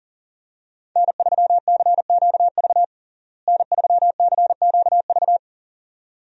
Morse code, heard as a keyed tone, about 40 wpm, 700 Hz. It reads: N3CQV N3CQV
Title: USA Call Signs - 5 Characters - Repeated 2x - 40wpm